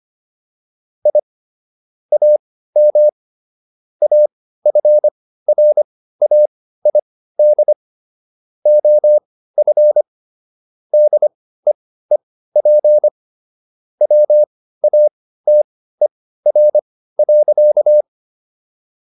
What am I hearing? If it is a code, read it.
Morse code, heard as a keyed tone, about 25 wpm, spaced as for 15 wpm, 600 Hz. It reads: I AM AFRAID OF DEEP WATER.